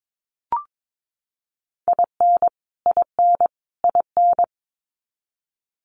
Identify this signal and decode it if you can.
Morse code, heard as a keyed tone, about 22 wpm, 700 Hz. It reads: IN IN IN
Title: US States Repeated 3x 22wpm